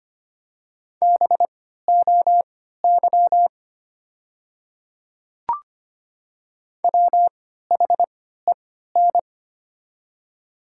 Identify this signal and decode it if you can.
Morse code, heard as a keyed tone, about 25 wpm, spaced as for 14 wpm, 700 Hz. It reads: BOY WHEN
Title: 3x Character Spacing - Top 300 Words - 25wpm